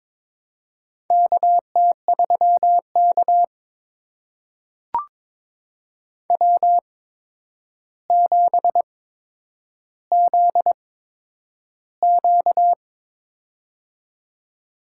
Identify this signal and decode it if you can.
Morse code, heard as a keyed tone, about 22 wpm, spaced as for 9 wpm, 700 Hz. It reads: KT3K W7ZQ
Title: USA Call Signs - 4 Characters - 8x Character Spacing - 22wpm